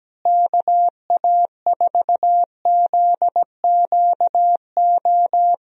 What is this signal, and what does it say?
Morse code, heard as a keyed tone, about 17 wpm, 700 Hz. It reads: KA4ZQO
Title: USA Call Signs - 6 Characters - Repeated 3x - 17wpm